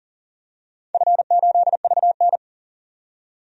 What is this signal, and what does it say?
Morse code, heard as a keyed tone, about 40 wpm, 700 Hz. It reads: F8VN
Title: DXCC-100 Call Signs - 4 Characters - 40wpm